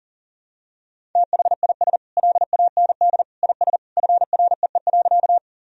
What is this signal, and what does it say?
Morse code, heard as a keyed tone, about 40 wpm, 700 Hz. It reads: THIS LAND IS FREE.
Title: Sentences from Top 1000 Words - Review Part 1 - 40wpm